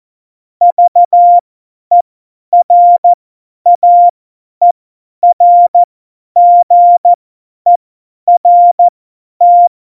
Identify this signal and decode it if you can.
Morse code, heard as a keyed tone, about 14 wpm, spaced as for 10 wpm, 700 Hz. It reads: VERAERGERT
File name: Morsekurs / Witze / Witz 24 April 2018, 50 F70